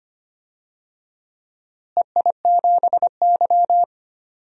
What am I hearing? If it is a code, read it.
Morse code, heard as a keyed tone, about 25 wpm, 700 Hz. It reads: EI7Y
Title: DXCC-100 Call Signs - 4 Characters - Repeated 2x - 25wpm